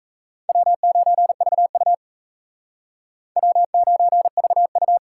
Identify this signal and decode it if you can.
Morse code, heard as a keyed tone, about 35 wpm, 700 Hz. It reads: W9VU W9VU
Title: Speed Racing - USA 1x2 Call Signs 28wpm